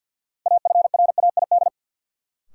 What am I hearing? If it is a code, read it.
Morse code, heard as a keyed tone, about 51 wpm, 700 Hz. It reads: AFRAID
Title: Sentences from Top 1500 Words - New Word Drill Alt - 50wpm